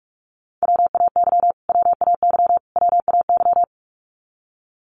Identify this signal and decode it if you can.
Morse code, heard as a keyed tone, about 44 wpm, 700 Hz. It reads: WAY WAY WAY
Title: Top 500 Words Repeated 45wpm